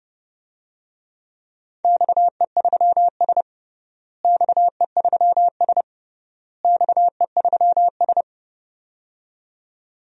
Morse code, heard as a keyed tone, about 30 wpm, 700 Hz. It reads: XE3S XE3S XE3S
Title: DXCC-100 Call Signs - 4 Characters - Repeated 3x - 30wpm